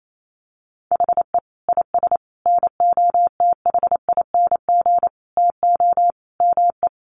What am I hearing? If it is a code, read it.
Morse code, heard as a keyed tone, about 28 wpm, 700 Hz. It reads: HE IS NOTHING TO ME
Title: Sentences from Top 1200 Words 28wpm